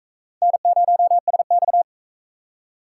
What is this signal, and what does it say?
Morse code, heard as a keyed tone, about 39 wpm, 700 Hz. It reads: N0SX